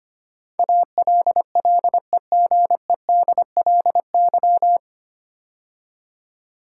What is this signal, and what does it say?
Morse code, heard as a keyed tone, about 25 wpm, 700 Hz. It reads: ALLEGEDLY
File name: practice / 25wpm / 9 Letter Words - First 3 Letters Given - 25wpm